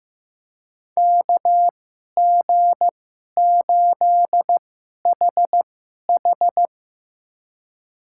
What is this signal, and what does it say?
Morse code, heard as a keyed tone, about 15 wpm, spaced as for 11 wpm, 700 Hz. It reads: KG8HH
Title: USA Call Signs - 2x Character Spacing - 15wpm